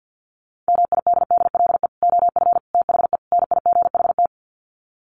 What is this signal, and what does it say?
Morse code, heard as a keyed tone, about 49 wpm, 700 Hz. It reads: MIDDLE OF THE NIGHT
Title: Sets of 4 Words 50wpm